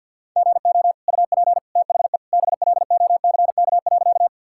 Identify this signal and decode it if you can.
Morse code, heard as a keyed tone, about 50 wpm, 700 Hz. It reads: GO UP THE BLOCK.
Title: Sentences from Top 1600 Words 50wpm